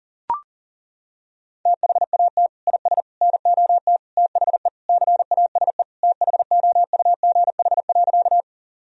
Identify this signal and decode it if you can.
Morse code, heard as a keyed tone, about 40 wpm, 700 Hz. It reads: THAT IS NOT THE CASE THOUGH.
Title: Sentences from Top 1000 Words - Review Part 2 - 40wpm